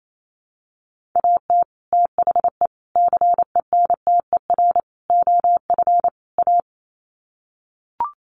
Morse code, heard as a keyed tone, about 28 wpm, 700 Hz. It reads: AT THE CENTER OF A
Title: Sets of 5 Words Encore2 28wpm